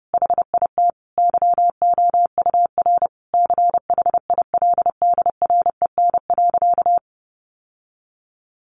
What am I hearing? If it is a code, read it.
Morse code, heard as a keyed tone, about 30 wpm, 700 Hz. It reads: HIT YOUR CHILDREN.
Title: Sentences from Top 1200 Words 30wpm